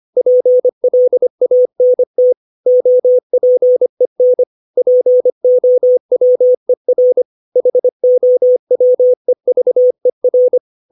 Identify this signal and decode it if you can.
Morse code, heard as a keyed tone, about 25 wpm, 500 Hz. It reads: PLANT OPEN POWER HOWEVER
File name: 632 words / 632 Words 25WPM